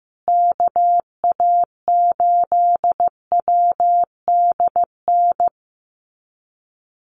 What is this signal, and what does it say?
Morse code, heard as a keyed tone, about 15 wpm, 700 Hz. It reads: KA8WDN
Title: USA Call Signs Repeated 3x 15wpm